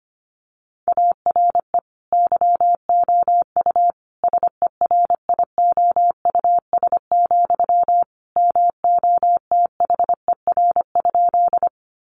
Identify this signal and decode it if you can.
Morse code, heard as a keyed tone, about 25 wpm, 700 Hz. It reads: ARE YOU SERIOUS, MOTHER?
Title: Sentences from Top 1100 Words 25wpm